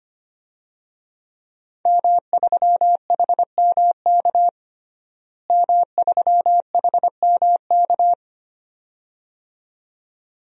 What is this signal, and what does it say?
Morse code, heard as a keyed tone, about 25 wpm, 700 Hz. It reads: M3HMK M3HMK
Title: DXCC-100 Call Signs - 5 Characters - Repeated 2x - 25wpm